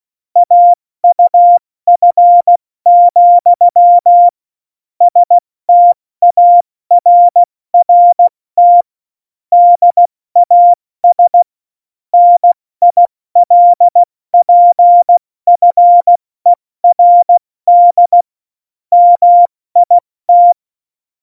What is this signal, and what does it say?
Morse code, heard as a keyed tone, about 16 wpm, 700 Hz. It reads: AUF, STARRT DAS NILPFERD MIT
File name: Morsekurs / Witze / Witz 08. Dezember 2020, 70 F80